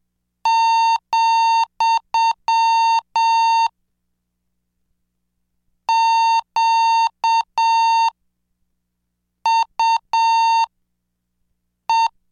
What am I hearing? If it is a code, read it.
Morse code, heard as a keyed tone, about 7 wpm, spaced as for 4 wpm, 910 Hz. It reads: , QUE
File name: cursos / CDestudo / 26 Track 26